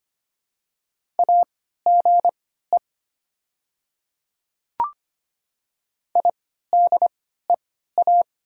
Morse code, heard as a keyed tone, about 25 wpm, spaced as for 14 wpm, 690 Hz. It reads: AGE IDEA
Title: 3x Character Spacing - Top 500 Words - 25wpm